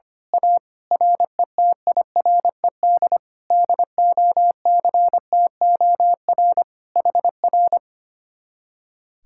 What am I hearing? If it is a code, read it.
Morse code, heard as a keyed tone, about 24 wpm, 700 Hz. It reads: A RETIRED DOCTOR HR
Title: Speed-Racing Common QSO Phrases - 25wpm